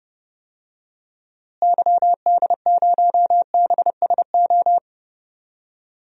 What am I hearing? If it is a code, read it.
Morse code, heard as a keyed tone, about 30 wpm, 700 Hz. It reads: YD0BSO
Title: DXCC-100 Call Signs - 6 Characters - Repeated 3x - 30wpm